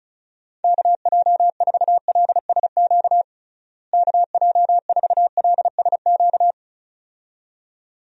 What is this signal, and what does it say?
Morse code, heard as a keyed tone, about 34 wpm, 700 Hz. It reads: KJ4LSQ KJ4LSQ
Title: USA Call Signs - 6 Characters - Repeated 2x - 35wpm